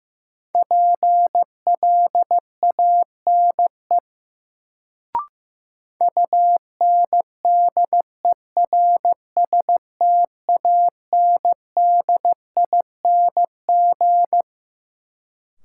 Morse code, heard as a keyed tone, about 15 wpm, 700 Hz. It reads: PLANE UNDERSTANDING
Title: Sentences from Top 1000 Words - Review Part 3 - New Word Drill - 15wpm